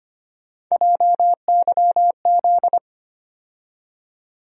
Morse code, heard as a keyed tone, about 25 wpm, 700 Hz. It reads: JYZ
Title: USA Call Sign Postfixes - 3 Letters 25wpm